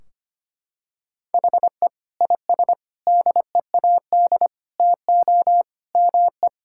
Morse code, heard as a keyed tone, about 25 wpm, 700 Hz. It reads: HE IS DEAD TO ME
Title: Sentences from Top 1600 Words 25wpm